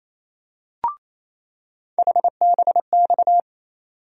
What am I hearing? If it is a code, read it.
Morse code, heard as a keyed tone, about 28 wpm, 700 Hz. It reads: HBX
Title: USA Call Sign Postfixes - 3 Letters 28wpm